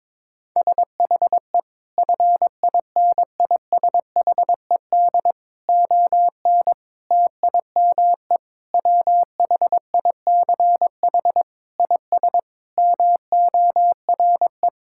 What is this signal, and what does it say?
Morse code, heard as a keyed tone, about 22 wpm, 700 Hz. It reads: SHE FINISHED ON TIME WHICH IS MORE
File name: practice / 22wpm / Sentences with 8-Word Phrases - 22wpm